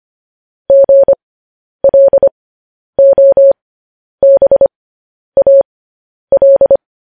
Morse code, heard as a keyed tone, about 25 wpm, spaced as for 10 wpm, 550 Hz. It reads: GLOBAL
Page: news headlines in morse code at 10 wpm